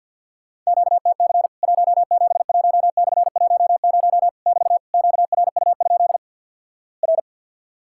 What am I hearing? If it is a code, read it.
Morse code, heard as a keyed tone, about 50 wpm, 700 Hz. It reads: QTC 171/10 <BT> QRU? R
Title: Worked All Europe DX Contest - QTC Exchange - 50wpm